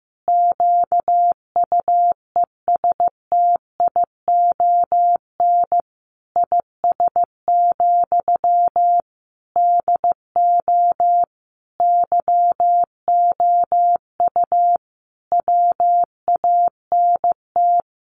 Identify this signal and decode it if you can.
Morse code, heard as a keyed tone, about 15 wpm, 700 Hz. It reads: QUESTION IS, DO YOU WANT
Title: Sentences from Top 1000 Words 15wpm